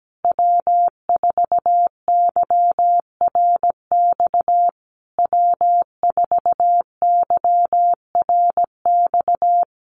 Morse code, heard as a keyed tone, about 17 wpm, 700 Hz. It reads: W4YRX W4YRX
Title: Common QSO Phrases Encore 17wpm